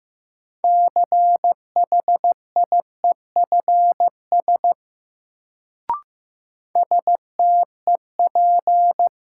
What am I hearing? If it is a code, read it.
Morse code, heard as a keyed tone, about 15 wpm, 700 Hz. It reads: CHIEFS STEP
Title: Sets of 5 Words Encore - New Word Drill - 15wpm